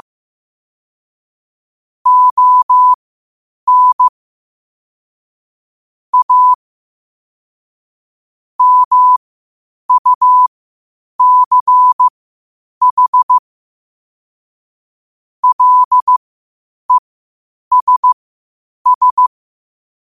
Morse code, heard as a keyed tone, about 15 wpm, spaced as for 8 wpm, 1000 Hz. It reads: ON A MUCH LESS
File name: qotdfiles / 8wpm